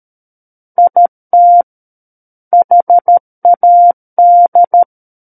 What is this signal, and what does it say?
Morse code, heard as a keyed tone, about 13 wpm, 700 Hz. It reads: IT HAD
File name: Book One / 04-Chapter 4 - 13 WPM